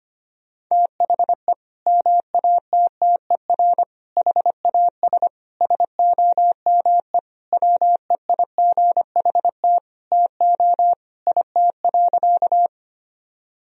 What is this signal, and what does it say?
Morse code, heard as a keyed tone, about 25 wpm, 700 Hz. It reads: THE MATTER HAS SOME WEIGHT TO IT.
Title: Sentences from Top 1000 Words - Review Part 3 - 25wpm